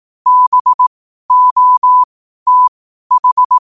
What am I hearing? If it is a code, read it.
Morse code, heard as a keyed tone, about 18 wpm, spaced as for 12 wpm, 1000 Hz. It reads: BOTH
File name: qotdfiles / 12wpm